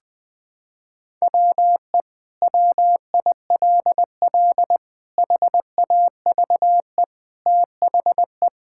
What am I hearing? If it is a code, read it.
Morse code, heard as a keyed tone, about 21 wpm, 700 Hz. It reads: WE WILL HAVE THE
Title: Sentences from Top 1000 Words - Review Part 6 - 20wpm